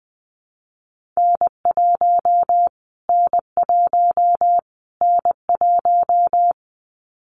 Morse code, heard as a keyed tone, about 20 wpm, 700 Hz. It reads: N1 N1 N1
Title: Callsign Prefixes for Top 25 DXCC - Repeated 3x - 20wpm